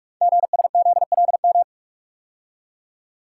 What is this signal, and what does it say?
Morse code, heard as a keyed tone, about 44 wpm, 690 Hz. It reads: GSZLM
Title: Code Groups - 5 Letters - 45wpm